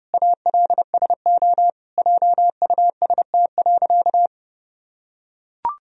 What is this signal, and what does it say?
Morse code, heard as a keyed tone, about 30 wpm, 700 Hz. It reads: ALSO JUST.
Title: Sentences from Top 1000 Words - Review Part 1 - 30wpm